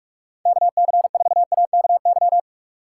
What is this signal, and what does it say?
Morse code, heard as a keyed tone, about 64 wpm, spaced as for 54 wpm, 700 Hz. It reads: KC4AKY
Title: Speed Racing - USA 2x3 Call Signs 45wpm